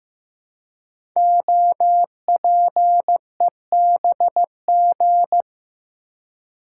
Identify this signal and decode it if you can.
Morse code, heard as a keyed tone, about 15 wpm, 700 Hz. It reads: OPEBG
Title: Code Groups - 5 Letter:Numbers - 15wpm